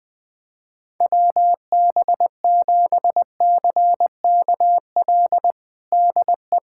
Morse code, heard as a keyed tone, about 20 wpm, 700 Hz. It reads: WB7CKL DE
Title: USA Callsign DE Callsign - 6 Characters - 20wpm